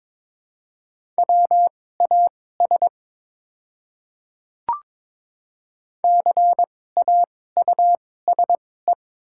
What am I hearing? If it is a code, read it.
Morse code, heard as a keyed tone, about 22 wpm, spaced as for 16 wpm, 700 Hz. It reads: WAS CAUSE
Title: 2x Character Spacing - Top 200 Words - 22wpm